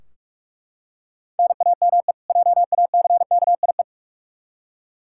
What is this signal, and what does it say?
Morse code, heard as a keyed tone, about 42 wpm, 700 Hz. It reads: NAME JACKIE
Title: Practice QSO - Exchange 1 - Beginning - 45wpm